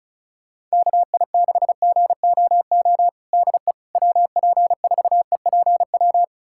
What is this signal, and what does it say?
Morse code, heard as a keyed tone, about 36 wpm, 690 Hz. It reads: KI6GOO DE WP4EPW
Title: USA Callsign DE Callsign - 6 Characters - 35wpm